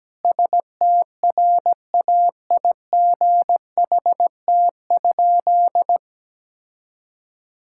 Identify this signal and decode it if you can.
Morse code, heard as a keyed tone, about 17 wpm, 700 Hz. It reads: STRAIGHT?
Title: Sentences from Top 1000 Words - Review Part 6 - 17wpm